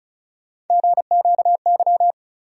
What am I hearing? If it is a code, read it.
Morse code, heard as a keyed tone, about 36 wpm, 700 Hz. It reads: GQY
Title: USA Call Sign Postfixes - 3 Letters 35wpm